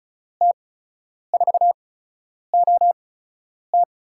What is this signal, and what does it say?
Morse code, heard as a keyed tone, about 36 wpm, spaced as for 10 wpm, 700 Hz. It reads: T4OT
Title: USA Call Signs - 5 Characters - 8x Character Spacing - 35wpm